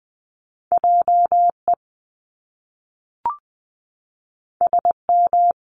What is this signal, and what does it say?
Morse code, heard as a keyed tone, about 19 wpm, 700 Hz. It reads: JE SM
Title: Top Two Letter Combinations in Words 20wpm